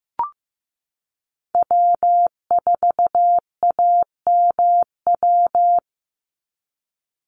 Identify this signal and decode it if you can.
Morse code, heard as a keyed tone, about 15 wpm, 700 Hz. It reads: W4AMW
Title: NA SOTA Chaser Callsigns 15wpm